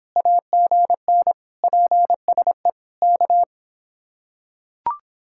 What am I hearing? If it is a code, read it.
Morse code, heard as a keyed tone, about 26 wpm, 700 Hz. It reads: AGN PSE K